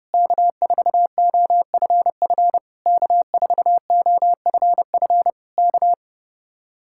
Morse code, heard as a keyed tone, about 28 wpm, 700 Hz. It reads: K4OFF K4OFF K
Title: Speed-Racing Common QSO Phrases - 30wpm